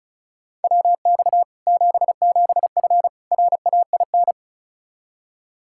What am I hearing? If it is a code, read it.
Morse code, heard as a keyed tone, about 36 wpm, 700 Hz. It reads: WX 77F RAIN
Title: Practice QSO - Exchange 2 - Info - 35wpm